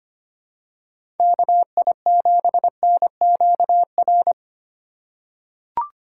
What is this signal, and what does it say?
Morse code, heard as a keyed tone, about 25 wpm, 700 Hz. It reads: KI7NQR
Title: USA Call Signs - 6 Characters - 3x Character Spacing - 25wpm